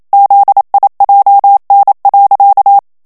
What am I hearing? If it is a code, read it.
Morse code, heard as a keyed tone, about 27 wpm, 800 Hz. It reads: ZIJN.
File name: HOLLANDAIS texte clair 1 vitesse 26 mots minute